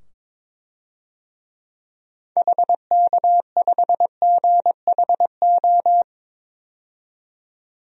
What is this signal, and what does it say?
Morse code, heard as a keyed tone, about 22 wpm, 700 Hz. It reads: HK5GHO
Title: DXCC-100 Call Signs - 6 Characters - 22wpm